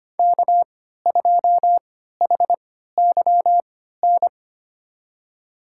Morse code, heard as a keyed tone, about 25 wpm, spaced as for 14 wpm, 700 Hz. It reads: K2HYN
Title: USA Call Signs - 5 Characters - 3x Character Spacing - 25wpm